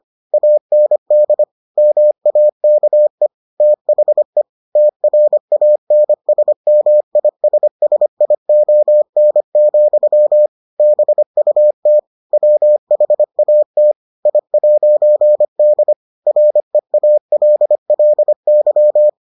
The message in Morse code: AND MAKE THE TRANSMISSION, BUT WHAT I'D REALLY

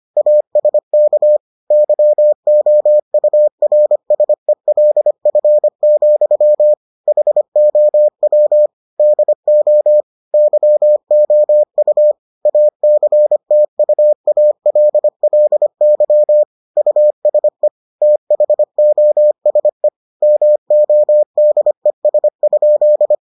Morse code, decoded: ASK YOURSELF, HOW DO YOU ACTUALLY USE THOSE MODES?

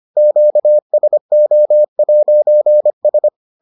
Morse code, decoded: QSO'S

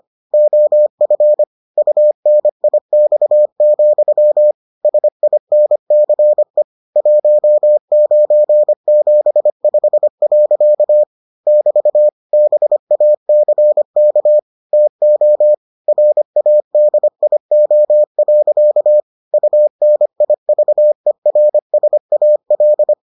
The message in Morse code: OF UNIX, SINCE 1975. <BT> BACK TO RADIO. UNIVERSAL

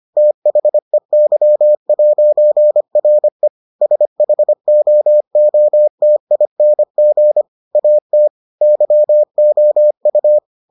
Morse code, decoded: THEY'RE SHOOTING AT YOU

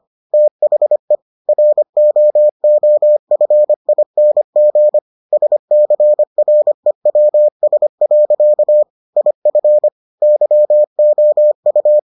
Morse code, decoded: THE ROOFING SCREWS. IF YOU

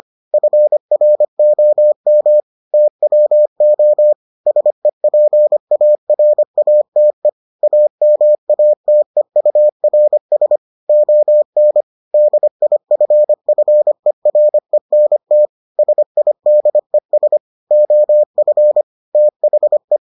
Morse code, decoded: FROM TWO SEPARATE AMATEURS ON DIFFERENT SIDES OF THE

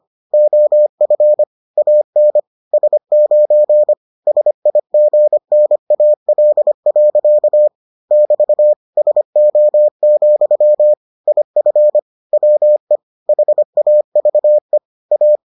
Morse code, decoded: OF AN S9 SIGNAL. <BT> SO, IF WE HAVE A

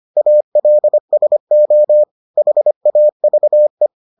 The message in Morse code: ALSO HAVE

